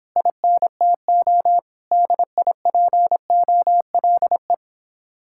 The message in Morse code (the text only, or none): INTO DIPOLE